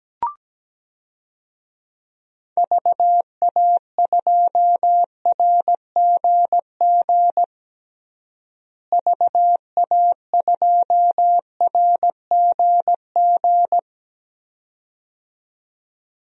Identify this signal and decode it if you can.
Morse code, heard as a keyed tone, about 17 wpm, 700 Hz. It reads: VA2RGG VA2RGG
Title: DXCC-100 Call Signs - 6 Characters - Repeated 2x - 17wpm